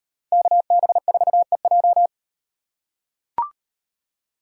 Morse code, heard as a keyed tone, about 38 wpm, 700 Hz. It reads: KB4EJ